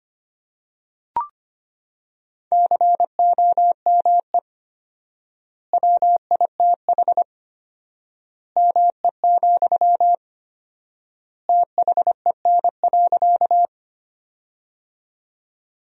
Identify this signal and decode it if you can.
Morse code, heard as a keyed tone, about 25 wpm, 700 Hz. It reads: COME WITH ME, THEN.
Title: 4x Word-Spacing - Sentences from Top 200 Words - 25wpm